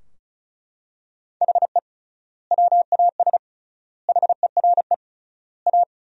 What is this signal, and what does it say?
Morse code, heard as a keyed tone, about 34 wpm, 700 Hz. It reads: HE WAS HERE A
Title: 3x Word-Spacing - Sentences from Top 500 Words - 35wpm